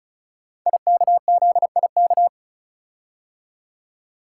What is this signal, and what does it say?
Morse code, heard as a keyed tone, about 36 wpm, 700 Hz. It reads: IKZIK
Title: Code Groups - 5 Letters - 35wpm